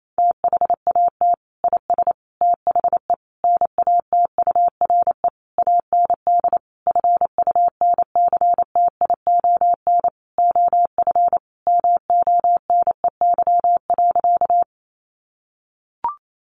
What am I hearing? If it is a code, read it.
Morse code, heard as a keyed tone, about 28 wpm, 700 Hz. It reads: THAT IS THE NATURE AND FUNCTION OF MONEY.
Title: Sentences from Top 1100 Words 28wpm